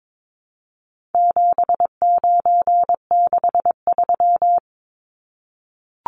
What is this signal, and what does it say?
Morse code, heard as a keyed tone, about 22 wpm, 700 Hz. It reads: 7963
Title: Numbers - 4-Digit 22wpm